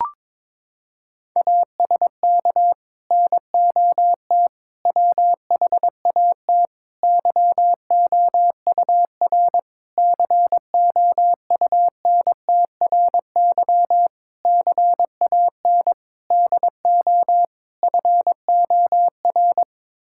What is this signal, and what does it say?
Morse code, heard as a keyed tone, about 22 wpm, 700 Hz. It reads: ASK NOT WHAT YOUR COUNTRY CAN DO FOR
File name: practice / 22wpm / Sets of 8 Words - 22wpm